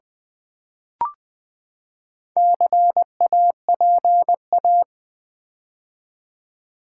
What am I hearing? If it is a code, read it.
Morse code, heard as a keyed tone, about 20 wpm, 700 Hz. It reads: CAPA
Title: Build A Word - 4 Letters At A Time - 10-Letter Words - 20wpm